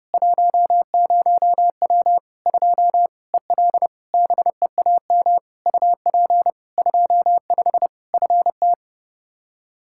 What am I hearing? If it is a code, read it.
Morse code, heard as a keyed tone, about 30 wpm, 700 Hz. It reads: 10W 2 EL BEAM UP 25 FT
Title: Practice QSO - Exchange 2 - Info - 30wpm